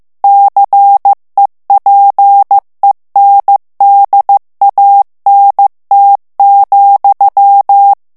Morse code, heard as a keyed tone, about 15 wpm, 800 Hz. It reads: CEPENDANT,
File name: morse mp3 / FRANCAIS TEXTE CLAIR 1 VITESSE 14 MOTS MINUTES